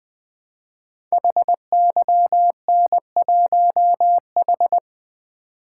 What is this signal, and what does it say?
Morse code, heard as a keyed tone, about 20 wpm, 700 Hz. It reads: HYN1H
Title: Code Groups - 5 Letter:Numbers - 20wpm